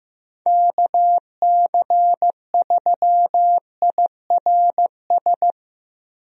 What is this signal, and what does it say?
Morse code, heard as a keyed tone, about 15 wpm, 700 Hz. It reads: KC3IRS
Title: USA Call Signs - 6 Characters - Repeated 3x - 15wpm